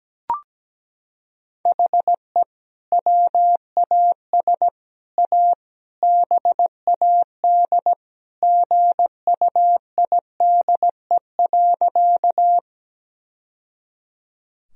Morse code, heard as a keyed tone, about 17 wpm, 700 Hz. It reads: HE WAS A BAD GUIDE.
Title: Sentences from Top 1000 Words - Review Part 3 - 17wpm